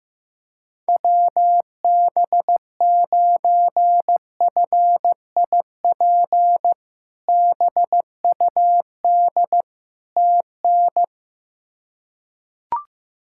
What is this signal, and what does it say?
Morse code, heard as a keyed tone, about 15 wpm, 700 Hz. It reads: WB9FIP BUD TN
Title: K1USN SST QSO Practice - 15wpm